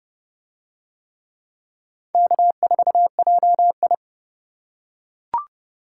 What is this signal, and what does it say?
Morse code, heard as a keyed tone, about 29 wpm, 700 Hz. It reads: K4JI